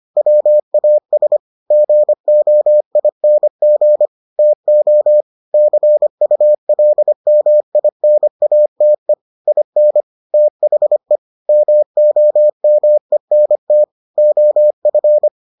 Morse code, heard as a keyed tone, about 25 wpm, 600 Hz. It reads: WAS GOING TO CULMINATE IN THE MOMENT OF